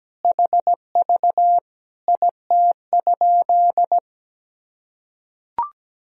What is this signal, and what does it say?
Morse code, heard as a keyed tone, about 17 wpm, 700 Hz. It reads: HV IT?
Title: Sentences from Top 100 Words and Abbreviations 17wpm